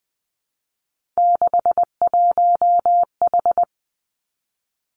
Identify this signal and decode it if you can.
Morse code, heard as a keyed tone, about 20 wpm, 700 Hz. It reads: 61H
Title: Three Challenging Letter-Number Combinations 20wpm